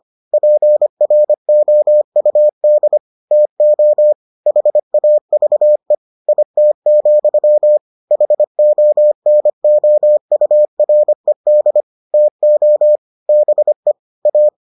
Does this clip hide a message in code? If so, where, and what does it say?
Yes: Morse code from 0.3 to 14.5 s, heard as a keyed tone, about 25 wpm, 600 Hz. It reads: PROUD TO HAVE IT, HONOURED TO BE A